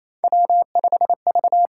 Morse code, heard as a keyed tone, about 28 wpm, 700 Hz. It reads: W5V
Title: USA Call Signs - 4 Characters - 8x Character Spacing - 28wpm